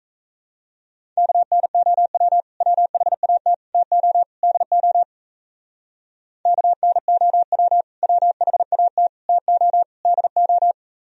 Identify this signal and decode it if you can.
Morse code, heard as a keyed tone, about 34 wpm, 700 Hz. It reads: KNOW WHAT TO DO KNOW WHAT TO DO
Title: Speed-Racing - Sets of 4 Words 28wpm